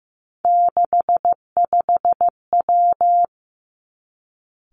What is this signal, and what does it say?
Morse code, heard as a keyed tone, about 15 wpm, 700 Hz. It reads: 65W
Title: Three Challenging Letter-Number Combinations 15wpm